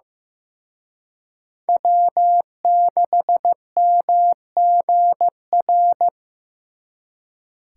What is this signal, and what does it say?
Morse code, heard as a keyed tone, about 18 wpm, 700 Hz. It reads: W6MGR
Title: Speed Racing - USA 1x3 Call Signs 15wpm